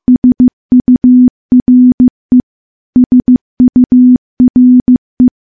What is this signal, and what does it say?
Morse code, heard as a keyed tone, about 15 wpm, 250 Hz. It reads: SURE SURE
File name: morse-training-15WPM